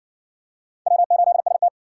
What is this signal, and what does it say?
Morse code, heard as a keyed tone, about 57 wpm, 700 Hz. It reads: W8ST